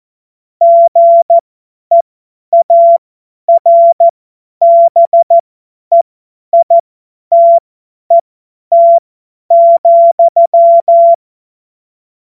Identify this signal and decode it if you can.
Morse code, heard as a keyed tone, about 14 wpm, spaced as for 10 wpm, 680 Hz. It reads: GEARBEITET,
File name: Witz-05-August-2025-50-F70